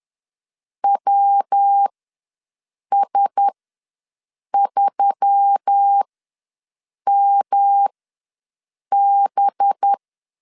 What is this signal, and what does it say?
Morse code, heard as a keyed tone, about 10 wpm, spaced as for 6 wpm, 800 Hz. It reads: WS3MB
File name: LEZIONI / 8 YJ45 / 8 YJ45 ws 10wpm 3843